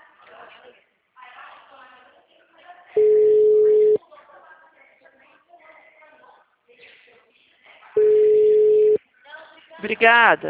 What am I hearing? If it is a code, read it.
Ringback tone (Europe, 425 Hz), with speech.
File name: out- -2012- - - .9851